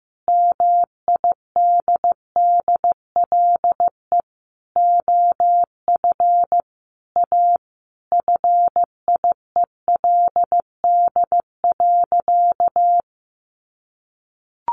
Morse code, heard as a keyed tone, about 15 wpm, 700 Hz. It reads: MIDDLE OF A FIELD.
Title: Sentences from Top 1000 Words 15wpm